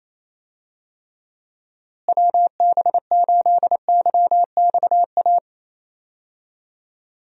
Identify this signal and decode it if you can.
Morse code, heard as a keyed tone, about 28 wpm, 700 Hz. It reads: WB8YXA
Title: USA Call Signs - 6 Characters - Repeated 2x - 28wpm